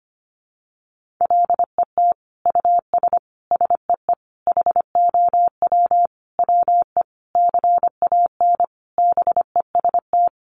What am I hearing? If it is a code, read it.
Morse code, heard as a keyed tone, about 25 wpm, 700 Hz. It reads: LET US SEE HOW WE CAN BEST
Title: Sentences from Top 1400 Words 25wpm